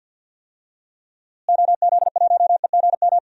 Morse code, heard as a keyed tone, about 47 wpm, 700 Hz. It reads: OZ1EGM
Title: DXCC-100 Call Signs - 6 Characters - Repeated 3x - 50wpm